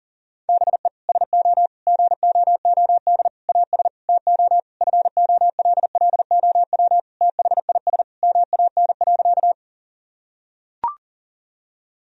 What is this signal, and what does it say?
Morse code, heard as a keyed tone, about 40 wpm, 700 Hz. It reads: BE SO GOOD AS TO FOLLOW THIS MAN.
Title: Sentences from Top 1000 Words - Review Part 1 - 40wpm